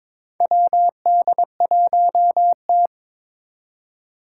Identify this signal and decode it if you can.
Morse code, heard as a keyed tone, about 22 wpm, 700 Hz. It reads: WD1T
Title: Four Challenging Letter-Number Combinations 22wpm